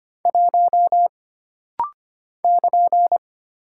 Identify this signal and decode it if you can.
Morse code, heard as a keyed tone, about 25 wpm, 700 Hz. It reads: 1 <KN>
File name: Single Character (Letter:Number:Punctuation:Prosign) - Rapid-Fire - 25wpm